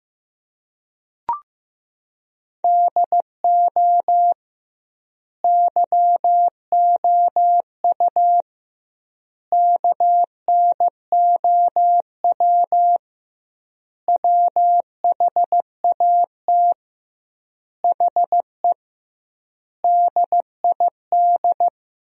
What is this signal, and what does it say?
Morse code, heard as a keyed tone, about 15 wpm, 700 Hz. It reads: DO YOU KNOW WHAT HE DID
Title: 2x Word-Spacing - Sentences from Top 300 Words - 15wpm